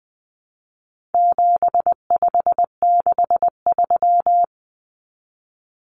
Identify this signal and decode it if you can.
Morse code, heard as a keyed tone, about 20 wpm, 700 Hz. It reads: 7563
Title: Numbers - 4-Digit 20wpm